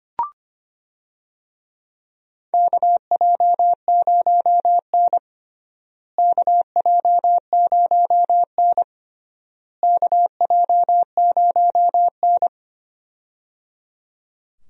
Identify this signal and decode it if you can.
Morse code, heard as a keyed tone, about 25 wpm, 700 Hz. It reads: KJ0N KJ0N KJ0N
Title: USA Call Signs - 4 Characters - Repeated 3x - 25wpm